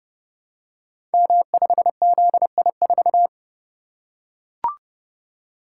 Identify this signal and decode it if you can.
Morse code, heard as a keyed tone, about 30 wpm, 700 Hz. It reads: M5ZI4
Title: Code Groups - 5 Letter:Numbers - 30wpm